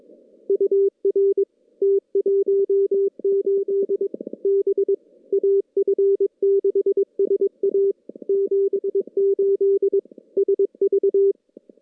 Morse code, heard as a keyed tone, about 22 wpm, 390 Hz. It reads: UR T18 B AF6SA 78 SV
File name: contests / CW-SS-k5tr- -2300